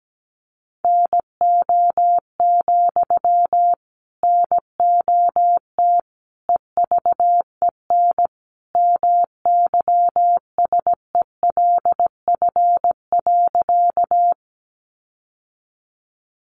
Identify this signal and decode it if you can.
Morse code, heard as a keyed tone, about 17 wpm, 700 Hz. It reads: NO, NOT EVEN MYSELF.
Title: Sentences from Top 1300 Words 17wpm